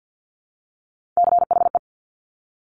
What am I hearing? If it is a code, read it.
Morse code, heard as a keyed tone, about 50 wpm, 700 Hz. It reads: CHE
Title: Top Three Letter Combinations in Words 50wpm